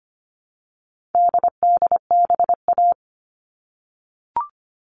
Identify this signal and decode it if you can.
Morse code, heard as a keyed tone, about 25 wpm, 700 Hz. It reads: DDBA
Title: All Cut Numbers - 4-Digit 25wpm